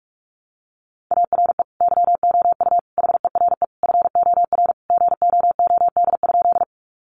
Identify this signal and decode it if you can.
Morse code, heard as a keyed tone, about 44 wpm, 700 Hz. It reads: ARE YOU HERE FOR GOOD?
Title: Sentences from Top 200 Words 45wpm